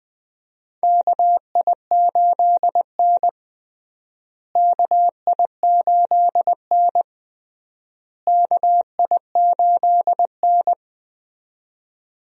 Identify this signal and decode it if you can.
Morse code, heard as a keyed tone, about 21 wpm, 700 Hz. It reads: KI8N KI8N KI8N
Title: USA Call Signs - 4 Characters - Repeated 3x - 20wpm